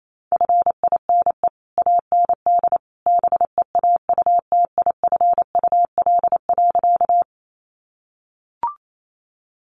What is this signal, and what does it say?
Morse code, heard as a keyed tone, about 28 wpm, 700 Hz. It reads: FINE AND BEAUTIFUL.
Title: Sentences from Top 900 Words 28wpm